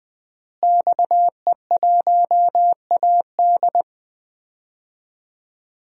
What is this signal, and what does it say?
Morse code, heard as a keyed tone, about 20 wpm, 700 Hz. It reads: XE1AD
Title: DXCC-100 Call Signs - 5 Characters - Repeated 2x - 20wpm